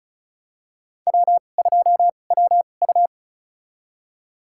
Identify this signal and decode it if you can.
Morse code, heard as a keyed tone, about 34 wpm, spaced as for 25 wpm, 700 Hz. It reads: W2WU
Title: USA Call Signs - 2x Character Spacing - 35wpm